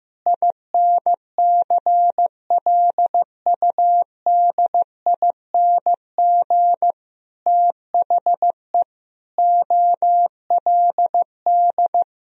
Morse code, heard as a keyed tone, about 15 wpm, 700 Hz. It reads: INCLUDING THE OLD
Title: Sentences from Top 1000 Words - Review Part 4 - 15wpm